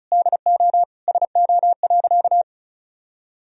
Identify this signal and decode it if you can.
Morse code, heard as a keyed tone, about 36 wpm, 700 Hz. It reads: DO SO.
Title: Sentences from Top 1000 Words - Review Part 4 - 35wpm